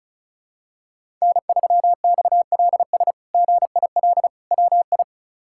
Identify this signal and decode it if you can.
Morse code, heard as a keyed tone, about 36 wpm, 700 Hz. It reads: N3XLS GIL WI
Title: K1USN SST QSO Practice - 35wpm